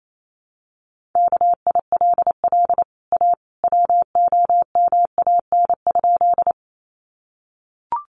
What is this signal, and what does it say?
Morse code, heard as a keyed tone, about 28 wpm, 700 Hz. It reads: KILL A WOMAN?
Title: Sentences from Top 1400 Words 28wpm